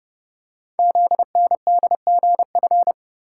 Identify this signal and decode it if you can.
Morse code, heard as a keyed tone, about 29 wpm, 700 Hz. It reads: ZNDGF
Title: Code Groups - 5 Letters - 30wpm